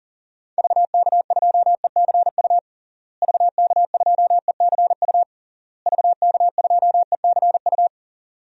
Morse code, heard as a keyed tone, about 37 wpm, 700 Hz. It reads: VK2ECU VK2ECU VK2ECU
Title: DXCC-100 Call Signs - 6 Characters - Repeated 3x - 40wpm